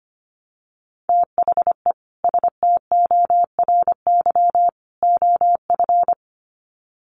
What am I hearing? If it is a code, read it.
Morse code, heard as a keyed tone, about 25 wpm, 700 Hz. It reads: THE STORY OF
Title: Sets of 3 Words 25wpm